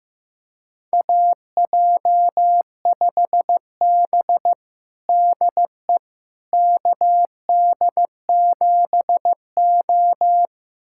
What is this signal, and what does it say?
Morse code, heard as a keyed tone, about 15 wpm, 700 Hz. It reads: AJ5B DE KD7O
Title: USA Callsign DE Callsign - 4 Characters - 15wpm